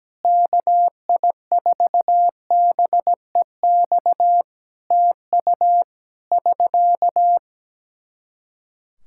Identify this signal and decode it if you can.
Morse code, heard as a keyed tone, about 17 wpm, 700 Hz. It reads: KI4BEX TU <SK>
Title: K1USN SST QSO Practice 17wpm